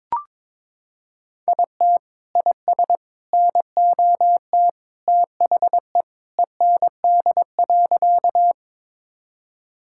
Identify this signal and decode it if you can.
Morse code, heard as a keyed tone, about 22 wpm, 700 Hz. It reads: IT IS NOT THE END.
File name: practice / 22wpm / Sentences with 3-Word Phrases - 22wpm